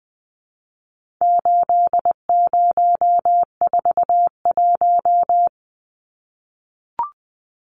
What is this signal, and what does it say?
Morse code, heard as a keyed tone, about 20 wpm, 700 Hz. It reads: 8041
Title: Numbers - 4-Digit 20wpm